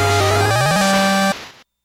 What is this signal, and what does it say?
There is no code.